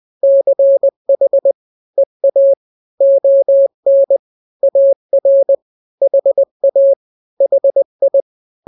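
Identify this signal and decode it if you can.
Morse code, heard as a keyed tone, about 20 wpm, 550 Hz. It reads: CH EA ON AR HA HI
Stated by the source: Morse drill of two-letter n-grams at 20/18 WPM (Farnsworth), from KK7RR.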